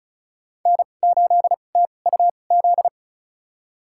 Morse code, heard as a keyed tone, about 36 wpm, spaced as for 25 wpm, 700 Hz. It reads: N8TUZ
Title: USA Call Signs - 5 Characters - 2x Character Spacing - 35wpm